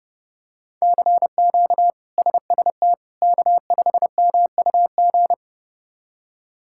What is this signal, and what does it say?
Morse code, heard as a keyed tone, about 30 wpm, 700 Hz. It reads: CQ SST K5MUG
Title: K1USN SST QSO Practice - 30wpm